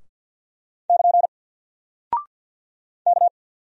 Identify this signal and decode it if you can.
Morse code, heard as a keyed tone, about 48 wpm, 700 Hz. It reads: <KN> K
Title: Single Character (Letter:Number:Punctuation:Prosign) Alt 50wpm